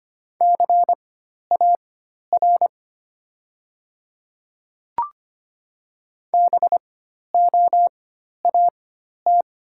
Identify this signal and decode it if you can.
Morse code, heard as a keyed tone, about 25 wpm, spaced as for 12 wpm, 700 Hz. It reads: CAR BOAT
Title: 4x Character Spacing - Top 500 Words - 25wpm